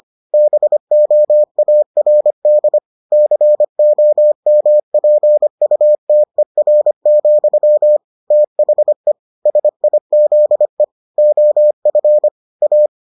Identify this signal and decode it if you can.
Morse code, heard as a keyed tone, about 25 wpm, 600 Hz. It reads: BOARD COMPUTER, THE SIZE OF A